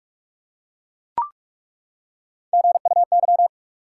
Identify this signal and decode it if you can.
Morse code, heard as a keyed tone, about 42 wpm, 700 Hz. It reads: GUY
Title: Three Letter Words - 45wpm